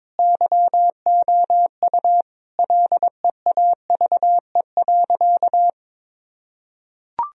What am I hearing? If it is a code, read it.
Morse code, heard as a keyed tone, about 22 wpm, 700 Hz. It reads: YOU LEAVE.
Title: Sentences from Top 1000 Words - Review Part 4 - 22wpm